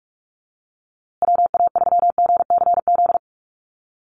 Morse code, heard as a keyed tone, about 44 wpm, 700 Hz. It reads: WA3GCZ
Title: USA Call Signs Repeated 3x 45wpm